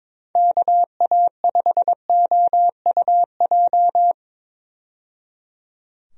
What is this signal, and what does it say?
Morse code, heard as a keyed tone, about 22 wpm, 700 Hz. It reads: KA5OUJ
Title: USA Call Signs - 6 Characters - 22wpm